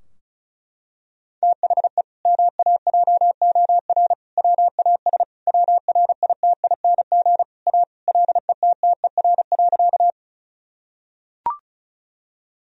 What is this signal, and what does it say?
Morse code, heard as a keyed tone, about 36 wpm, 700 Hz. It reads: THE MAJOR WAS WRITING A LETTER.
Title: Sentences from Top 1000 Words - Review Part 4 - 35wpm